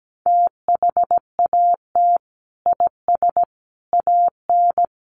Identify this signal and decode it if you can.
Morse code, heard as a keyed tone, about 17 wpm, 700 Hz. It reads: THAT IS AN
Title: Sentences from Top 1300 Words 17wpm